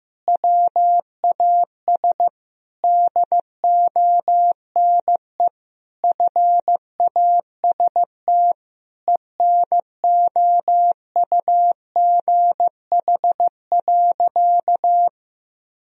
Morse code, heard as a keyed tone, about 15 wpm, 700 Hz. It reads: WAS DONE FAST ENOUGH.
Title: Sentences from Top 1000 Words - Review Part 2 - 15wpm